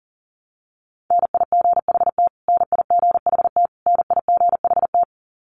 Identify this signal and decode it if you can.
Morse code, heard as a keyed tone, about 40 wpm, 700 Hz. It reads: NIGHT NIGHT NIGHT
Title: Top 500 Words Repeated 40wpm